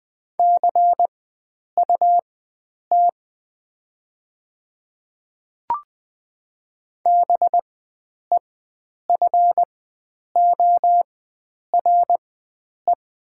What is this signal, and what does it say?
Morse code, heard as a keyed tone, about 20 wpm, spaced as for 9 wpm, 700 Hz. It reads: CUT BEFORE
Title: 4x Character Spacing - Top 500 Words - 20wpm